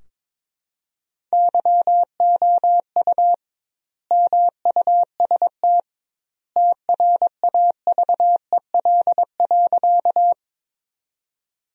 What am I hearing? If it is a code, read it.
Morse code, heard as a keyed tone, about 22 wpm, 700 Hz. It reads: YOU MUST TRAVEL.
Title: 2x Word-Spacing - Sentences from Top 400 Words - 22wpm